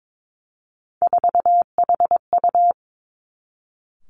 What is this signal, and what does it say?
Morse code, heard as a keyed tone, about 22 wpm, 700 Hz. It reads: 4HU
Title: Three Challenging Letter-Number Combinations 22wpm